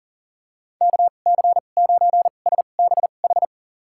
Morse code, heard as a keyed tone, about 40 wpm, spaced as for 29 wpm, 700 Hz. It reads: KC9SBH
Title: USA Call Signs - 2x Character Spacing - 40wpm